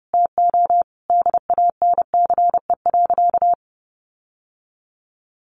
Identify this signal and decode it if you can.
Morse code, heard as a keyed tone, about 30 wpm, 700 Hz. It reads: TO DANCE.
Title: Sentences from Top 900 Words 30wpm